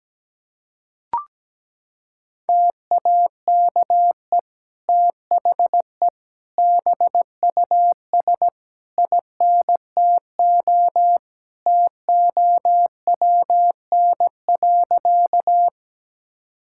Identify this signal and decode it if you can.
Morse code, heard as a keyed tone, about 17 wpm, 700 Hz. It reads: TAKE THE BUS INTO TOWN.
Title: Sentences from Top 1000 Words - Review Part 6 - 17wpm